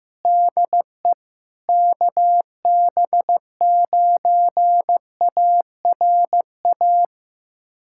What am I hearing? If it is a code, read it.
Morse code, heard as a keyed tone, about 15 wpm, 700 Hz. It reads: DE KB9ARA
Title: USA Callsign DE Callsign - 6 Characters - 15wpm